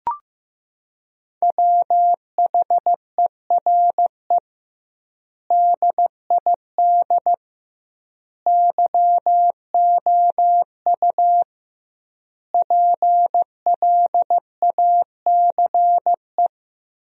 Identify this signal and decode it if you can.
Morse code, heard as a keyed tone, about 16 wpm, 700 Hz. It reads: WHERE DID YOU PLACE
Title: 2x Word-Spacing - Sentences from Top 200 Words - 15wpm